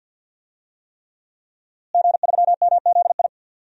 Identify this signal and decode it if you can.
Morse code, heard as a keyed tone, about 52 wpm, 700 Hz. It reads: G3MZI
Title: DXCC-100 Call Signs - 5 Characters - Repeated 3x - 50wpm